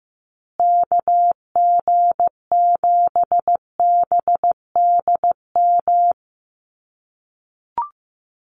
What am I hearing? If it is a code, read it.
Morse code, heard as a keyed tone, about 15 wpm, 700 Hz. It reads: KG7BDM
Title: USA Call Signs Repeated 2x 15wpm